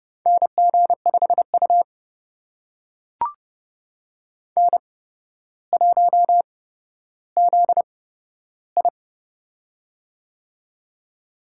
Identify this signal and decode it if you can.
Morse code, heard as a keyed tone, about 31 wpm, spaced as for 13 wpm, 700 Hz. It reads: NG5U N1ZI
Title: USA Call Signs - 4 Characters - 8x Character Spacing - 30wpm